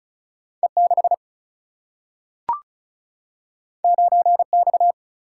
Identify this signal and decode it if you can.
Morse code, heard as a keyed tone, about 36 wpm, 700 Hz. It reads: E 6 9 X
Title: Single-Letter Single-Number 35wpm